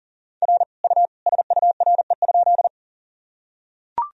Morse code, heard as a keyed tone, about 40 wpm, 700 Hz. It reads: R U SURE?
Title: Sentences from Top 300 Words and Abbreviations 40wpm